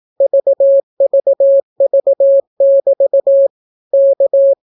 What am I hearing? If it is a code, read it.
Morse code, heard as a keyed tone, about 18 wpm, 550 Hz. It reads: VVV<BT> K